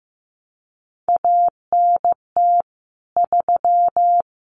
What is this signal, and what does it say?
Morse code, heard as a keyed tone, about 15 wpm, 700 Hz. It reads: ANT 3
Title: Common QSO Phrases 15wpm